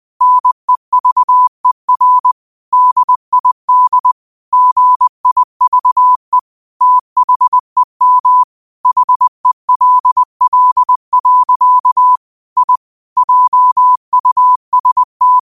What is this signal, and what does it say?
Morse code, heard as a keyed tone, about 20 wpm, 1000 Hz. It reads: NEVER DID GIVE THEM HELL. I JUST